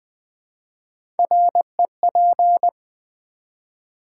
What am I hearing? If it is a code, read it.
Morse code, heard as a keyed tone, about 20 wpm, 700 Hz. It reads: REP
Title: Build A Word - 1 Letter At A Time - 3 to 5-Letter Words - 20wpm